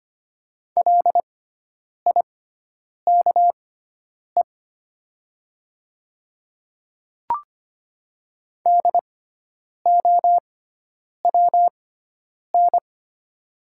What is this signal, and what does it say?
Morse code, heard as a keyed tone, about 25 wpm, spaced as for 9 wpm, 700 Hz. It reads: LIKE DOWN
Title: 6x Character Spacing - Top 200 Words - 25wpm